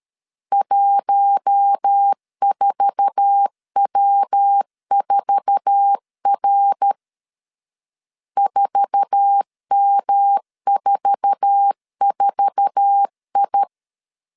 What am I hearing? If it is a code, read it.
Morse code, heard as a keyed tone, about 13 wpm, 800 Hz. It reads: 14W4R 4M44I
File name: LEZIONI / 8 YJ45 / 8 YJ4 12wpm 2151